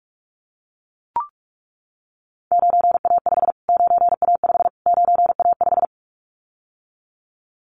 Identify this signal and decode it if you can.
Morse code, heard as a keyed tone, about 44 wpm, 700 Hz. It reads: 9A5 9A5 9A5
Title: Callsign Prefixes for Top 25 DXCC - Repeated 3x - 45wpm